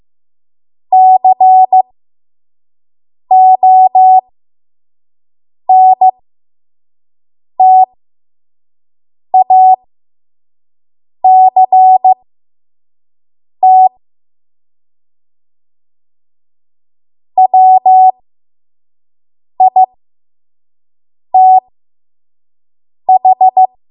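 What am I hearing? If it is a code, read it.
Morse code, heard as a keyed tone, about 15 wpm, spaced as for 5 wpm, 750 Hz. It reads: CONTACT WITH
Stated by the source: 5 WPM Morse code practice.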